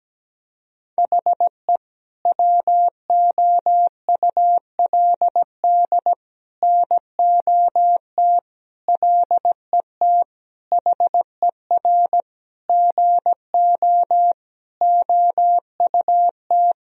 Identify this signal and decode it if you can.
Morse code, heard as a keyed tone, about 17 wpm, 700 Hz. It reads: HE WOULD NOT LET HER GO OUT
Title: Sentences from Top 1000 Words - Review Part 2 - 17wpm